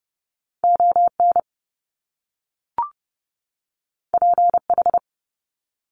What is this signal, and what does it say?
Morse code, heard as a keyed tone, about 30 wpm, 700 Hz. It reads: ON PH
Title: Top Two Letter Combinations in Words 30wpm